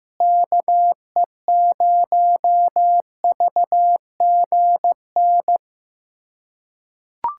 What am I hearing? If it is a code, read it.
Morse code, heard as a keyed tone, about 16 wpm, 700 Hz. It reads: KE0VGN
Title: USA Callsign DE Callsign 15wpm